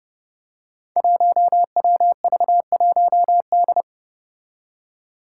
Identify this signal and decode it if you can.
Morse code, heard as a keyed tone, about 30 wpm, 700 Hz. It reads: 1WV1D
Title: Five Challenging Letter-Number Combinations 30wpm